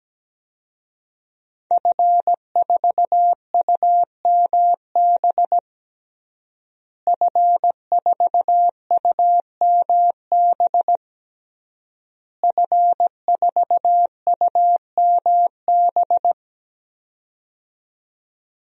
Morse code, heard as a keyed tone, about 17 wpm, 700 Hz. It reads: F4UMB F4UMB F4UMB
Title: DXCC-100 Call Signs - 5 Characters - Repeated 3x - 17wpm